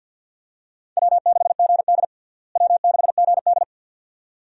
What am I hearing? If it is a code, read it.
Morse code, heard as a keyed tone, about 49 wpm, 700 Hz. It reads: W6GD W6GD
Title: USA Call Signs - 4 Characters - Repeated 2x - 50wpm